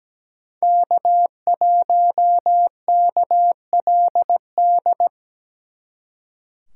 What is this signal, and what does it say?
Morse code, heard as a keyed tone, about 20 wpm, 700 Hz. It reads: K1KLD
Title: Speed Racing - USA 1x3 Call Signs 17wpm